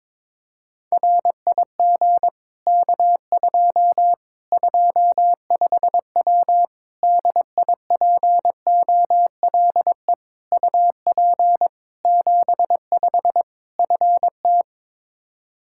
Morse code, heard as a keyed tone, about 22 wpm, 700 Hz. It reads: RIG K2 25W DIPOLE UP 75 FT
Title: Practice QSO - Exchange 2 - Info - 22wpm